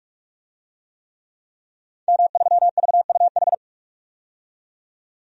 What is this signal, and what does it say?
Morse code, heard as a keyed tone, about 44 wpm, 700 Hz. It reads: M3VUH
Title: DXCC-100 Call Signs - 5 Characters - 45wpm